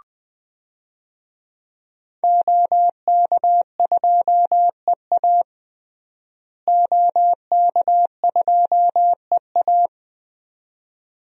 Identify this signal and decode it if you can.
Morse code, heard as a keyed tone, about 20 wpm, 700 Hz. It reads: OK2EA OK2EA
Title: DXCC-100 Call Signs - 5 Characters - Repeated 2x - 20wpm